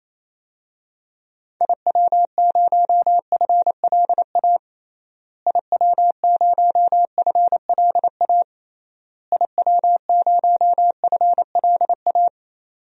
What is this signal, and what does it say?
Morse code, heard as a keyed tone, about 27 wpm, 700 Hz. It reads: IW0FLA IW0FLA IW0FLA
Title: DXCC-100 Call Signs - 6 Characters - Repeated 3x - 28wpm